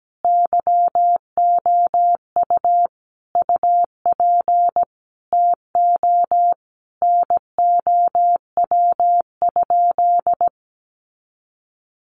Morse code, heard as a keyed tone, about 17 wpm, 700 Hz. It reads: YOU UP TO NOW?
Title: Sentences from Top 100 Words 17wpm